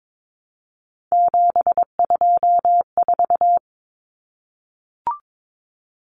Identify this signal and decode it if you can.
Morse code, heard as a keyed tone, about 22 wpm, 700 Hz. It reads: 724
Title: Numbers - 3-Digit 22wpm